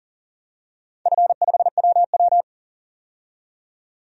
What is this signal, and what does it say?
Morse code, heard as a keyed tone, about 40 wpm, 700 Hz. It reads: F5WW
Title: DXCC-100 Call Signs - 4 Characters - 40wpm